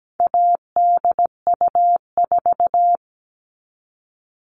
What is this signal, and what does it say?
Morse code, heard as a keyed tone, about 17 wpm, 700 Hz. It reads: ADU4
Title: All Cut Numbers - 4-Digit 17wpm